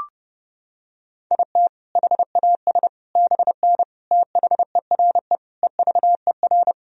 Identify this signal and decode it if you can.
Morse code, heard as a keyed tone, about 30 wpm, 700 Hz. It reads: IT HAS BN THERE EVER
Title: Sentences from Top 400 Words and Abbreviations 30wpm